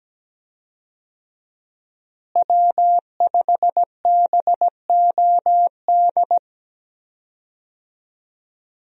Morse code, heard as a keyed tone, about 17 wpm, 700 Hz. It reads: W5BOD
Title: USA Call Signs - 5 Characters - Repeated 3x - 17wpm